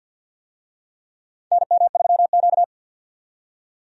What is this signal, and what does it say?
Morse code, heard as a keyed tone, about 53 wpm, 700 Hz. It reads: NM3Q
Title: USA Call Signs - 4 Characters - 50wpm